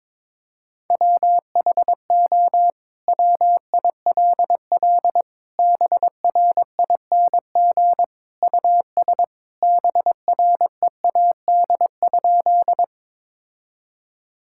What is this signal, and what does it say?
Morse code, heard as a keyed tone, about 22 wpm, 700 Hz. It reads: WHO WILL BRING US BREAD?
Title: Sentences from Top 1600 Words 22wpm